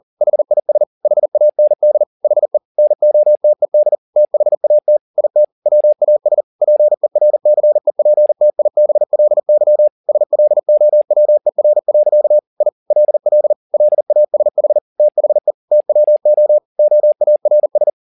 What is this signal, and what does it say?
Morse code, heard as a keyed tone, about 40 wpm, 600 Hz. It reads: HIS HAND HE NOTED THAT IT WAS PERCEPTIBLY SLOWER. I LL LASH THE TWO OARS
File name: the old man and the sea 2 40wpm